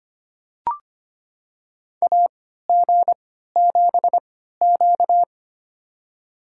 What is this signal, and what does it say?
Morse code, heard as a keyed tone, about 25 wpm, spaced as for 14 wpm, 700 Hz. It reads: AG7Q
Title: USA Call Signs - 3x Character Spacing - 25wpm